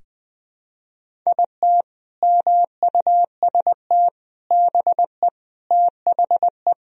Morse code, heard as a keyed tone, about 20 wpm, 700 Hz. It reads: IT MUST BE THE
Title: Sentences from Top 1000 Words - Review Part 1 - 20wpm